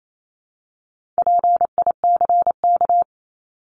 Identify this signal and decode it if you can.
Morse code, heard as a keyed tone, about 28 wpm, 700 Hz. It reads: PICK
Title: Top 751-1000 Words 28wpm